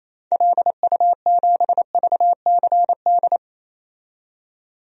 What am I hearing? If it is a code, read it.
Morse code, heard as a keyed tone, about 28 wpm, 700 Hz. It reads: LU7VCD
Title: DXCC-100 Call Signs - 6 Characters - 28wpm